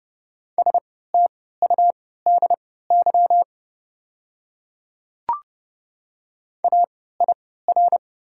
Morse code, heard as a keyed tone, about 30 wpm, spaced as for 17 wpm, 700 Hz. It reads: STUDY AIR
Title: 3x Character Spacing - Top 300 Words - 30wpm